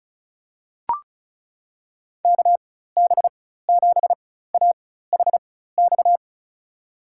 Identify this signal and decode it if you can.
Morse code, heard as a keyed tone, about 34 wpm, spaced as for 16 wpm, 700 Hz. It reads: KB7AHX
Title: USA Call Signs - 6 Characters - 4x Character Spacing - 35wpm